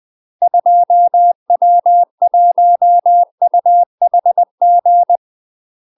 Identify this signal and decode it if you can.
Morse code, heard as a keyed tone, about 20 wpm, 700 Hz. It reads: 2W1UHG
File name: UK MASTER QSOs TRAINING / 20WPM / UK MASTER QSOs 0046